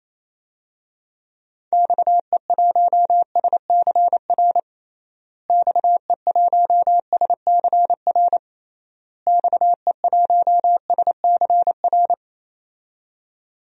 Morse code, heard as a keyed tone, about 28 wpm, 700 Hz. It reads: XE1SCR XE1SCR XE1SCR
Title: DXCC-100 Call Signs - 6 Characters - Repeated 3x - 28wpm